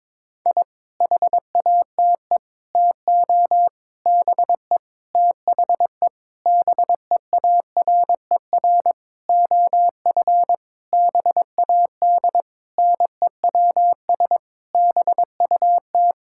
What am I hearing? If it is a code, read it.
Morse code, heard as a keyed tone, about 22 wpm, 700 Hz. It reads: I HATE TO BE THE BEARER OF BAD NEWS BUT
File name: Sentences with 9-Word Phrases - 22wpm